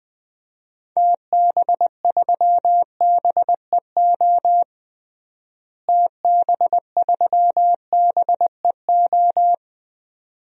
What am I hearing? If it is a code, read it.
Morse code, heard as a keyed tone, about 20 wpm, 700 Hz. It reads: TB3BEO TB3BEO
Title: DXCC-100 Call Signs - 6 Characters - Repeated 2x - 20wpm